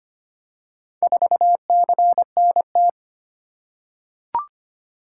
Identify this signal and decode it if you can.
Morse code, heard as a keyed tone, about 25 wpm, 700 Hz. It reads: 4CNT
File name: practice / 25wpm / Four Letter-Number Combinations - 25wpm